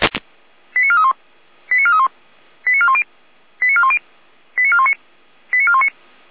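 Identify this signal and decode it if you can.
Five-tone selective call (ZVEI), heard as quick runs of five tones, tones of about 70 ms. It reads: 89431 89431 89410 89410 89419 89419
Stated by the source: ZVEI five-tone selective call received on an SDR.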